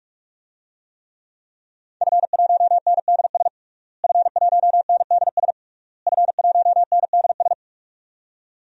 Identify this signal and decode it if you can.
Morse code, heard as a keyed tone, about 41 wpm, 700 Hz. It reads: F1NDS F1NDS F1NDS
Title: DXCC-100 Call Signs - 5 Characters - Repeated 3x - 45wpm